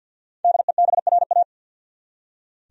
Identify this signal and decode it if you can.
Morse code, heard as a keyed tone, about 47 wpm, 700 Hz. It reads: DEBRA
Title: USA Common Names - Max 5 Letters - 50wpm